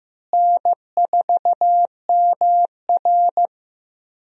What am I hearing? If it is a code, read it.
Morse code, heard as a keyed tone, about 15 wpm, 700 Hz. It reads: N4MR
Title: USA Call Signs - 4 Characters - Repeated 3x - 15wpm